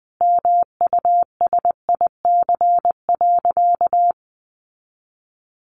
Morse code, heard as a keyed tone, about 20 wpm, 700 Hz. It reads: MUSIC.
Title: Sentences from Top 1200 Words 20wpm